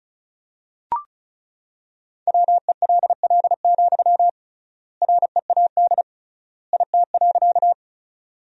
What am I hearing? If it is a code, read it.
Morse code, heard as a keyed tone, about 36 wpm, 700 Hz. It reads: WELL, READ IT.
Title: 3x Word-Spacing - Sentences from Top 200 Words - 35wpm